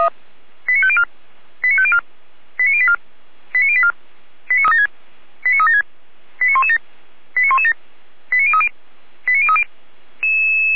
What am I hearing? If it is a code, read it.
Five-tone selective call (ZVEI), heard as quick runs of five tones, tones of about 70 ms. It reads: 89504 89504 89084 89084 89376 89117 89117 89030 89030
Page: ZVEI